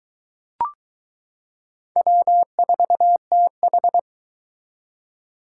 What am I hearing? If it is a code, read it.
Morse code, heard as a keyed tone, about 22 wpm, 700 Hz. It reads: W4TH